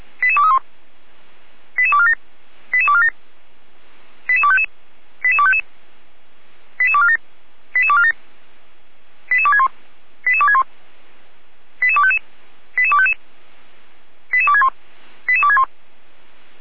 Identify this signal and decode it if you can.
Five-tone selective call (ZVEI), heard as quick runs of five tones, tones of about 70 ms. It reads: 88231 88247 88247 88255 88255 88247 88247 88261 88261 88255 88255 88261 88261
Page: /455/SDR/ZVEI/88231/2021.11.15/